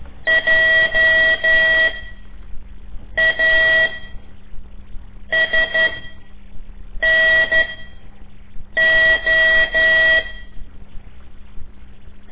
Morse code, heard as a keyed tone, about 12 wpm, spaced as for 5 wpm, 1950 Hz. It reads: JASNO
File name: ik1weg2 / CORSI CW / lez5-1